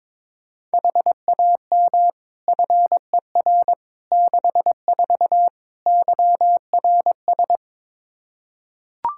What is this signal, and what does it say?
Morse code, heard as a keyed tone, about 22 wpm, 700 Hz. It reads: HAM FER 64 YRS
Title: K1USN SST QSO Practice 22wpm